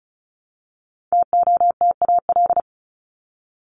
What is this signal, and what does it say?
Morse code, heard as a keyed tone, about 34 wpm, 700 Hz. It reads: TOTAL
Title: LY Suffix - Top 100 Words with Base Words 35wpm